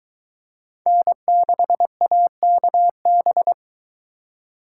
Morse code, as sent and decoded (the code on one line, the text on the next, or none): -. -.... .- -.- -...
N6AKB